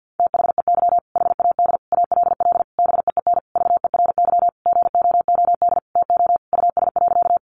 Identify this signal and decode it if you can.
Morse code, heard as a keyed tone, about 50 wpm, 700 Hz. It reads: THEY HAD ALL BEEN VERY GOOD TO US.
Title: Sentences from Top 200 Words 50wpm